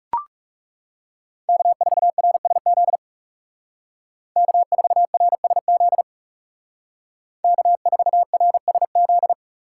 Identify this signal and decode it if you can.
Morse code, heard as a keyed tone, about 39 wpm, 700 Hz. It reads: K4RSZ K4RSZ K4RSZ